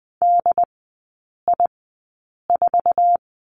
Morse code, heard as a keyed tone, about 20 wpm, spaced as for 8 wpm, 700 Hz. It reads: DI4
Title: 067 Lesson 12e - Letter d - Three Characters 20wpm